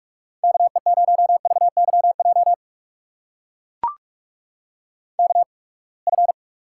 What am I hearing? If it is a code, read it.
Morse code, heard as a keyed tone, about 42 wpm, spaced as for 30 wpm, 700 Hz. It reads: KE0VYJ KF
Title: USA Call Signs - 6 Characters - 8x Character Spacing - 45wpm